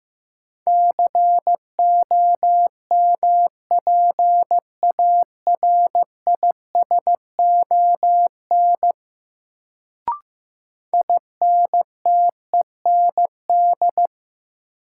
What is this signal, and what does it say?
Morse code, heard as a keyed tone, about 15 wpm, 700 Hz. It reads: COMPARISON INTEND
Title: Sentences from Top 1600 Words - New Word Drill - 15wpm